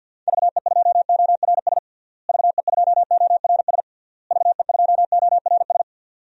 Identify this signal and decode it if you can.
Morse code, heard as a keyed tone, about 49 wpm, 700 Hz. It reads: VE2ORS VE2ORS VE2ORS
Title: DXCC-100 Call Signs - 6 Characters - Repeated 3x - 50wpm